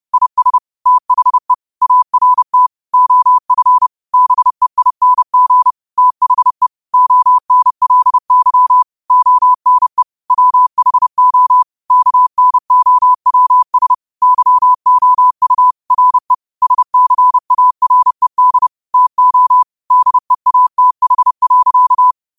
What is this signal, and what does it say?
Morse code, heard as a keyed tone, about 30 wpm, 1000 Hz. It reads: IS THE ART OF BEING THE ONLY ONE WHO KNOWS YOU RE SCARED TO DEATH.